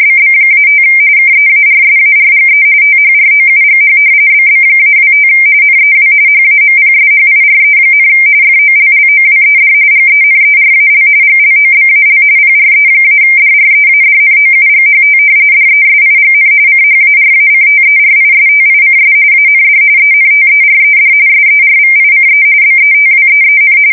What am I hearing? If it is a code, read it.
RTTY, about 45.45 baud. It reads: BB-5 KEARSARGE THROUGH BB-25 NEW HAMPSHIRE, FOLLOWED GENERAL GLOBAL PRE-DREADNOUGHT DESIGN CHARACTERISTICS AND ENTERED SERVICE BETWEEN
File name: fsk170 / battleships